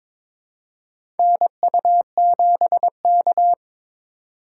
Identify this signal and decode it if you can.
Morse code, heard as a keyed tone, about 22 wpm, 700 Hz. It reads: NU7K
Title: USA Call Signs - 3x Character Spacing - 22wpm